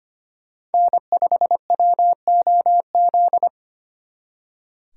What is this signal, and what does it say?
Morse code, heard as a keyed tone, about 33 wpm, 700 Hz. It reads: N5WOZ